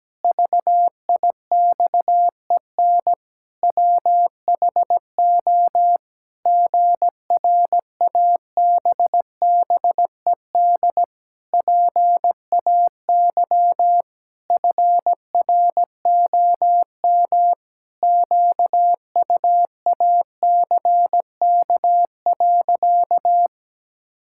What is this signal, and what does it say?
Morse code, heard as a keyed tone, about 17 wpm, 700 Hz. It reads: VIXEN WHO GRABBED PAY FROM QUACK.
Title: Pangram Challenge v2 - 17wpm